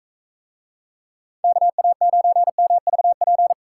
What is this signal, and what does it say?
Morse code, heard as a keyed tone, about 42 wpm, 700 Hz. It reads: KA9MVP